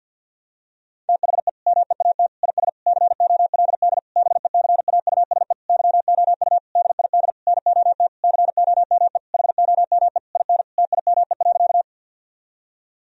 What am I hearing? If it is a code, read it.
Morse code, heard as a keyed tone, about 49 wpm, 700 Hz. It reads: THE MEAT IS COLD BECAUSE YOU DID NOT COME HOME IN TIME.
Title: Sentences from Top 1000 Words - Review Part 3 - 50wpm